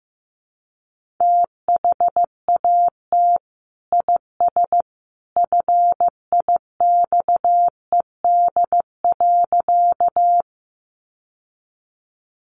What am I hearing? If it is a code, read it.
Morse code, heard as a keyed tone, about 15 wpm, 700 Hz. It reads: THAT IS FIXED.
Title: Sentences from Top 1100 Words 15wpm